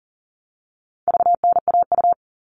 Morse code, heard as a keyed tone, about 40 wpm, 700 Hz. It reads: VNAU
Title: All Cut Numbers - 4-Digit 40wpm